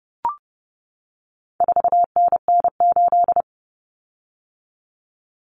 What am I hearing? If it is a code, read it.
Morse code, heard as a keyed tone, about 30 wpm, 700 Hz. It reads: 4NN8
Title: Common Cut Numbers - 4-Digit 30wpm